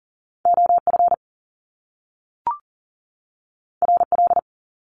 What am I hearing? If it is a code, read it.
Morse code, heard as a keyed tone, about 41 wpm, 700 Hz. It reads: OF RL
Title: Top Two Letter Combinations in Words 40wpm